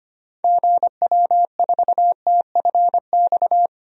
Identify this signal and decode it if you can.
Morse code, heard as a keyed tone, about 25 wpm, 700 Hz. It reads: GW4TFX
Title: DXCC-100 Call Signs - 6 Characters - Repeated 2x - 25wpm